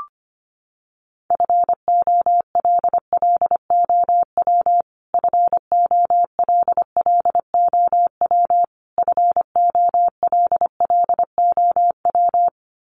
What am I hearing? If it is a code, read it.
Morse code, heard as a keyed tone, about 25 wpm, 700 Hz. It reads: FOLLOW FOLLOW FOLLOW
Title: Top 200 Words Repeated 25wpm